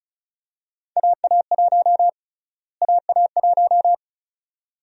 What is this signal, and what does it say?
Morse code, heard as a keyed tone, about 34 wpm, 700 Hz. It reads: AA1 AA1
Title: USA Call Sign Prefixes - Repeated 2x - 35wpm